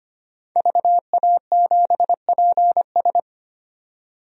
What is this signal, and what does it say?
Morse code, heard as a keyed tone, about 25 wpm, 700 Hz. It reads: VA7PS
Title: DXCC-100 Call Signs - 5 Characters - 25wpm